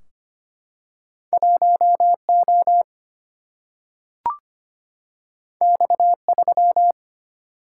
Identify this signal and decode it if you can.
Morse code, heard as a keyed tone, about 25 wpm, 700 Hz. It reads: 1O X3
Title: Single-Letter Single-Number 25wpm